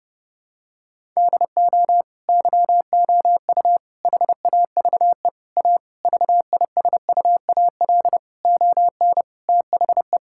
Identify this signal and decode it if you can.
Morse code, heard as a keyed tone, about 30 wpm, 700 Hz. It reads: DO YOU HAVE A VISUAL ON THE
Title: Sentences from Top 1000 Words - Review Part 6 - 30wpm